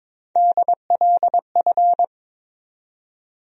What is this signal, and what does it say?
Morse code, heard as a keyed tone, about 22 wpm, 700 Hz. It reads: DLF
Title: USA Call Sign Postfixes - 3 Letters 22wpm